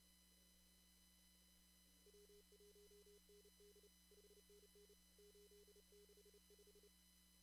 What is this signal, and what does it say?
Morse code, heard as a keyed tone, about 32 wpm, 410 Hz. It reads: W1ND 5NN 865
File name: CW / kt5j wpx cw-op-k5pi